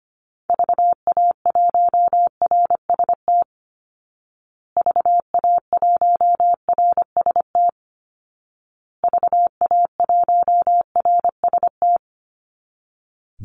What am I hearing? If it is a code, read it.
Morse code, heard as a keyed tone, about 25 wpm, 700 Hz. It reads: VA1RST VA1RST VA1RST
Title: Call Signs Repeated 2x - 25wpm